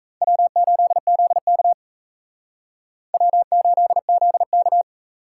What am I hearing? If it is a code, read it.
Morse code, heard as a keyed tone, about 37 wpm, 700 Hz. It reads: W8ZK W8ZK